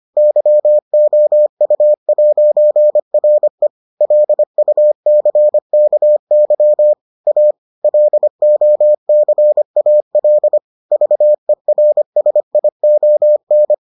Morse code, heard as a keyed tone, about 25 wpm, 600 Hz. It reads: YOU'RE LUCKY A LOCAL VERSION